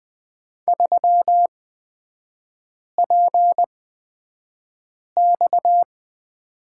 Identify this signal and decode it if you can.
Morse code, heard as a keyed tone, about 20 wpm, spaced as for 5 wpm, 700 Hz. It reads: 3PX
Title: 229 Lesson 37e - Letter x - Three Characters 20wpm